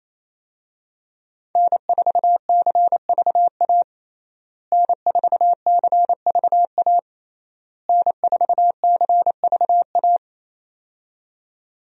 Morse code, heard as a keyed tone, about 28 wpm, 700 Hz. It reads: N4CVA N4CVA N4CVA
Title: USA Call Signs - 5 Characters - Repeated 3x - 28wpm